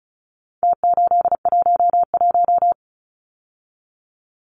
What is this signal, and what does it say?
Morse code, heard as a keyed tone, about 34 wpm, 700 Hz. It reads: T811
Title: Numbers - Contest Style - 4-Digit 35wpm